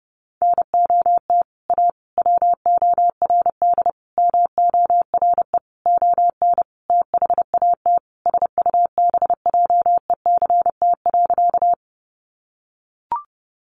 Morse code, heard as a keyed tone, about 30 wpm, 700 Hz. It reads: NOT A WORD MORE ON THAT SUBJECT.
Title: Sentences from Top 800 Words 30wpm